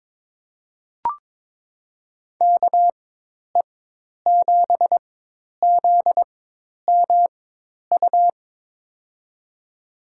Morse code, heard as a keyed tone, about 22 wpm, spaced as for 10 wpm, 700 Hz. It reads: KE7ZMU
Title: USA Call Signs - 6 Characters - 4x Character Spacing - 22wpm